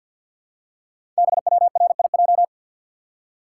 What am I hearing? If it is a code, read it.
Morse code, heard as a keyed tone, about 48 wpm, 700 Hz. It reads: BWRIJ
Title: Code Groups - 5 Letters - 50wpm